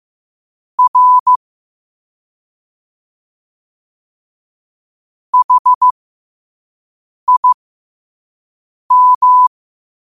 Morse code, heard as a keyed tone, about 15 wpm, spaced as for 5 wpm, 1000 Hz. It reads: R HIM
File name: qotdfiles / 5wpm